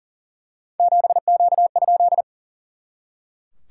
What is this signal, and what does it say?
Morse code, heard as a keyed tone, about 40 wpm, 690 Hz. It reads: 7Q?
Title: SOTA QSO Practice 40wpm